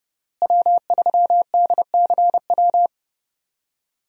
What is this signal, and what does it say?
Morse code, heard as a keyed tone, about 30 wpm, 700 Hz. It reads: W3DCW
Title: USA Call Signs - 5 Characters - Repeated 3x - 30wpm